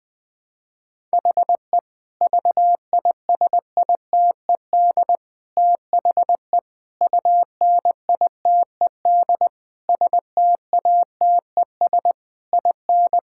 Morse code, heard as a keyed tone, about 20 wpm, 700 Hz. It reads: HE VISITED THE UNITED STATES IN
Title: Sentences with 4-Word Phrases - Spoken Beginning - 20wpm